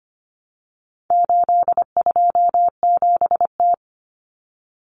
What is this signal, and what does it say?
Morse code, heard as a keyed tone, about 25 wpm, 700 Hz. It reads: 827T
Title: Common Cut Numbers - 4-Digit 25wpm